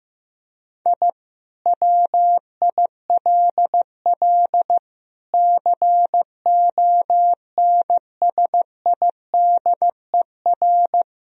Morse code, heard as a keyed tone, about 15 wpm, 700 Hz. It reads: I WILL CONSIDER
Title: Sentences from Top 1000 Words - Review Part 4 - 15wpm